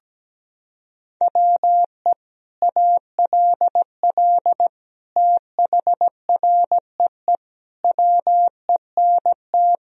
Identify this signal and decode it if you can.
Morse code, heard as a keyed tone, about 17 wpm, 700 Hz. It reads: WE ALL THREE WENT
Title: Sentences from Top 1000 Words - Review Part 1 - 17wpm